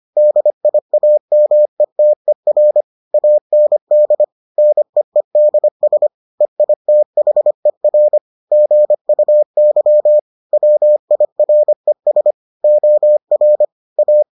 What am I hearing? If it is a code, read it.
Morse code, heard as a keyed tone, about 25 wpm, 600 Hz. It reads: DIAMETER AND NEEDS EITHER GUY WIRES OR A